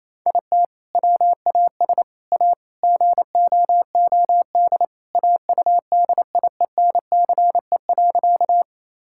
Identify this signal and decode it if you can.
Morse code, heard as a keyed tone, about 28 wpm, 700 Hz. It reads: IT WAS A GOOD AUDIENCE.
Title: Sentences from Top 1000 Words - Review Part 5 - 28wpm